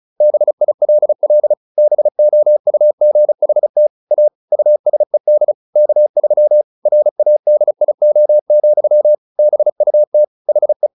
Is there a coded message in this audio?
Yes: Morse code from 0.2 to 10.9 s, heard as a keyed tone, about 34 wpm, 600 Hz. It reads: BILL BOUGHT A USED K3 RADIO, BUT HE